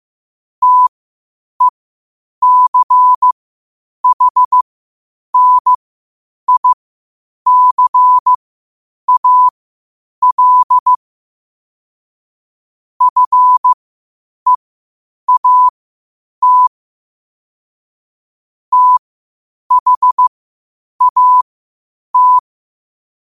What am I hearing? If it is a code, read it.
Morse code, heard as a keyed tone, about 15 wpm, spaced as for 8 wpm, 1000 Hz. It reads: TECHNICAL FEAT THAT
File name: qotdfiles / 8wpm